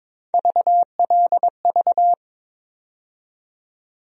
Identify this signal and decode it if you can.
Morse code, heard as a keyed tone, about 22 wpm, 700 Hz. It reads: VLV
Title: USA Call Sign Postfixes - 3 Letters 22wpm